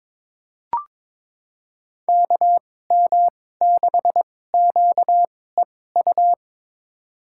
Morse code, heard as a keyed tone, about 22 wpm, spaced as for 16 wpm, 700 Hz. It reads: KM6QEU
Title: USA Call Signs - 6 Characters - 2x Character Spacing - 22wpm